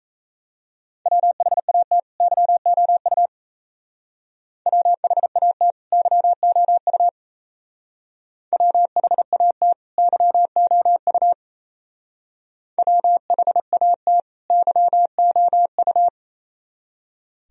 Morse code, heard as a keyed tone, about 34 wpm, 700 Hz. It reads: WHAT YOU WHAT YOU WHAT YOU WHAT YOU
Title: Speed-Racing - Sets of 2 Words 28wpm